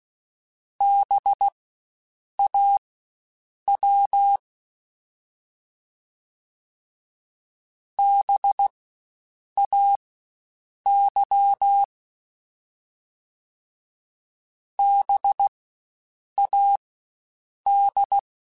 Morse code, heard as a keyed tone, about 16 wpm, spaced as for 7 wpm, 800 Hz. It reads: BAW BAY BAD
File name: Spaced Alphas / Spaced Alphas mp3s / Bs sp @15